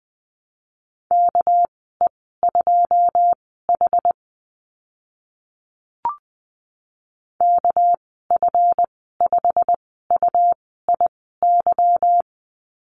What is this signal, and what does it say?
Morse code, heard as a keyed tone, about 20 wpm, spaced as for 15 wpm, 700 Hz. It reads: KE2H KF5UIY
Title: 231 Lesson 37g - Letter x - Call Signs 20wpm